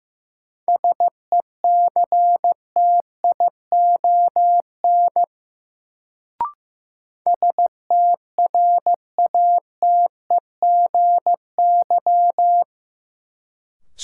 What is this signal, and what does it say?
Morse code, heard as a keyed tone, about 15 wpm, 700 Hz. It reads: SECTION STRATEGY
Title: Sentences from Top 1000 Words - Review Part 5 - New Word Drill - 15wpm